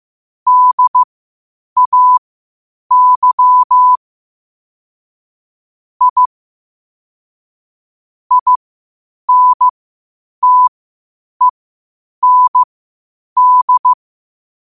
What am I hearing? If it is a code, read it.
Morse code, heard as a keyed tone, about 15 wpm, spaced as for 8 wpm, 1000 Hz. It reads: DAY I INTEND
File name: qotdfiles / 8wpm